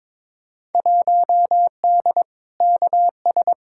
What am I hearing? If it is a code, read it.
Morse code, heard as a keyed tone, about 22 wpm, 700 Hz. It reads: 1D KS
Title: Field Day Exchanges - 22wpm